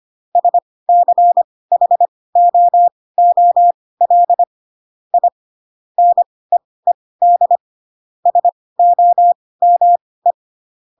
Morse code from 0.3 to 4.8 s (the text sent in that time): SCHOOL